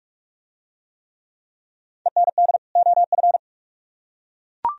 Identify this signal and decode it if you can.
Morse code, heard as a keyed tone, about 42 wpm, 700 Hz. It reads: END OF